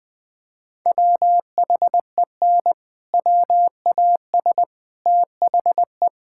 Morse code, heard as a keyed tone, about 20 wpm, 700 Hz. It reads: WHEN WAS THE
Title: Sentences with 7-Word Phrases - 20wpm